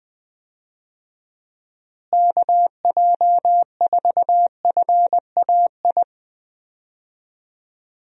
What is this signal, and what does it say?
Morse code, heard as a keyed tone, about 20 wpm, 700 Hz. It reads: KJ4FAI
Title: USA Call Signs - 6 Characters - Repeated 3x - 20wpm